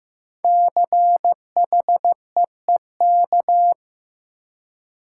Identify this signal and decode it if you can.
Morse code, heard as a keyed tone, about 15 wpm, 700 Hz. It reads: CHEEK
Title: Five Letter Words - 15wpm